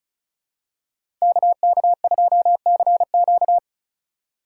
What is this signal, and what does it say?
Morse code, heard as a keyed tone, about 34 wpm, 700 Hz. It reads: KK2CQ
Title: USA Call Signs - 5 Characters - 6x Character Spacing - 35wpm